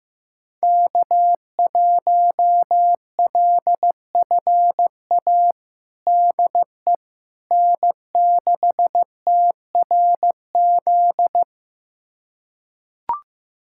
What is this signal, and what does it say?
Morse code, heard as a keyed tone, about 15 wpm, 700 Hz. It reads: K1LFA DE N6TRZ
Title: USA Callsign DE Callsign - 5 Characters - 15wpm